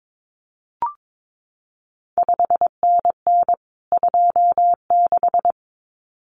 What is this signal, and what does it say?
Morse code, heard as a keyed tone, about 22 wpm, 700 Hz. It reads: 5NN 26
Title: CQ WW DX Contest Exchange Encore - 22wpm